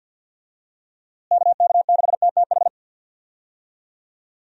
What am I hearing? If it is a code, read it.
Morse code, heard as a keyed tone, about 50 wpm, 700 Hz. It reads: KK6TTH
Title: USA Call Signs - 6 Characters - Repeated 2x - 50wpm